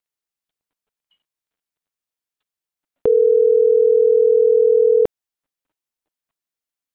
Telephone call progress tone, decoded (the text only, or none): ringback tone